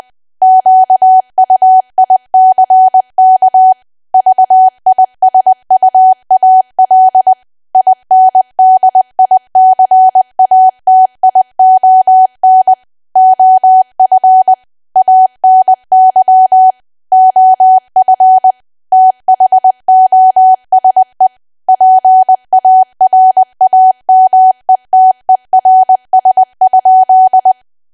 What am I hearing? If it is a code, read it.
Morse code, heard as a keyed tone, about 20 wpm, 750 Hz. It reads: QUICK VISUAL INDICATION OF ANY OF THOSE PARAMETERS?
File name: cwa / W1AW-Level3 / 20WPM